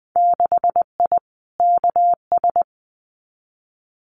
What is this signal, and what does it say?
Morse code, heard as a keyed tone, about 20 wpm, 700 Hz. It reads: 6I KS
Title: Winter Field Day Exchanges 20wpm